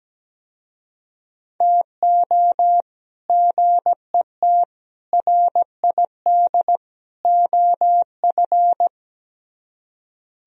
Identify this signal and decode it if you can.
Morse code, heard as a keyed tone, about 17 wpm, 700 Hz. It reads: TO GET RID OF
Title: Speed-Racing - Sets of 4 Words 17wpm